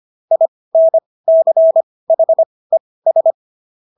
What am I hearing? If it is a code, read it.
Morse code, heard as a keyed tone, about 25 wpm, spaced as for 18 wpm, 650 Hz. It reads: INCHES